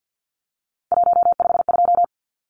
Morse code, heard as a keyed tone, about 50 wpm, 700 Hz. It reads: 152
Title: Numbers - 3-Digit 50wpm